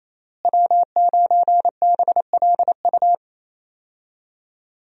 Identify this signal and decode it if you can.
Morse code, heard as a keyed tone, about 28 wpm, 700 Hz. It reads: W9BLU
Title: USA Call Signs - 5 Characters - Repeated 2x - 28wpm